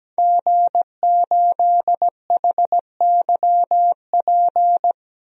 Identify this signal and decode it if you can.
Morse code, heard as a keyed tone, about 17 wpm, 700 Hz. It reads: G8HYP
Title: DXCC-100 Call Signs - 5 Characters - 17wpm